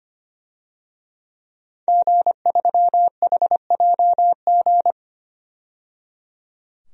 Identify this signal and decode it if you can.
Morse code, heard as a keyed tone, about 25 wpm, 700 Hz. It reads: G3HJG
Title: DXCC-100 Call Signs - 5 Characters - 25wpm